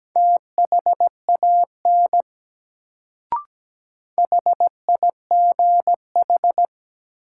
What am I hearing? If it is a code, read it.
Morse code, heard as a keyed tone, about 17 wpm, 700 Hz. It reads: THAN HIGH
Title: Sets of 2 Words - New Word Drill - 17wpm